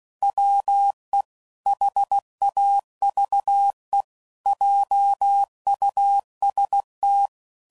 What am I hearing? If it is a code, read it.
Morse code, heard as a keyed tone, about 16 wpm, 800 Hz. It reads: WE HAVE JUST
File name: CW Code Groups @15WPM / Technical Subjects / OR Gate